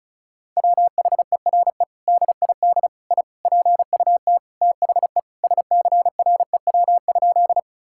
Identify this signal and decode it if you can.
Morse code, heard as a keyed tone, about 36 wpm, 700 Hz. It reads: WHERE DID I PUT THE SCREW?
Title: Sentences from Top 1600 Words 35wpm